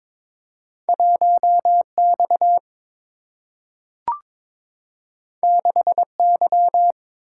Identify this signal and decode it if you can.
Morse code, heard as a keyed tone, about 22 wpm, 700 Hz. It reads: 1X 6Y
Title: Single-Letter Single-Number 22wpm